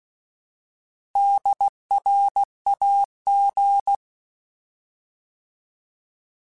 Morse code, heard as a keyed tone, about 16 wpm, 800 Hz. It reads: DRAG
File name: Alphabetical / Alphabetical mp3s / Ds @15